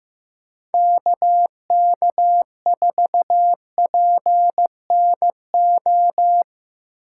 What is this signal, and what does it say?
Morse code, heard as a keyed tone, about 15 wpm, 700 Hz. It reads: KK4PNO
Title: USA Call Signs - 6 Characters - 2x Character Spacing - 15wpm